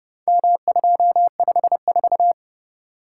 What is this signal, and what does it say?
Morse code, heard as a keyed tone, about 29 wpm, 700 Hz. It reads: M254